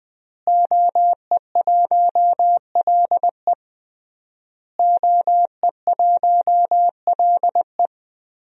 Morse code, heard as a keyed tone, about 21 wpm, 700 Hz. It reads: OE1LE OE1LE
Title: DXCC-100 Call Signs - 5 Characters - Repeated 3x - 20wpm